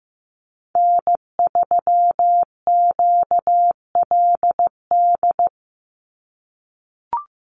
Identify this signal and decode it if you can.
Morse code, heard as a keyed tone, about 15 wpm, 700 Hz. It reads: N3QLD
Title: USA Call Signs 15wpm